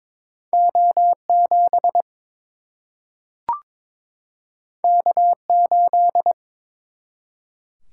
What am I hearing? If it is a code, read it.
Morse code, heard as a keyed tone, about 22 wpm, 700 Hz. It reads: O 7 K 8
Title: Single-Letter Single-Number 22wpm